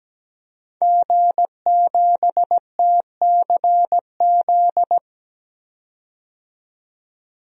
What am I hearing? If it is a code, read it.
Morse code, heard as a keyed tone, about 17 wpm, 700 Hz. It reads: G7TCZ
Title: DXCC-100 Call Signs - 5 Characters - 17wpm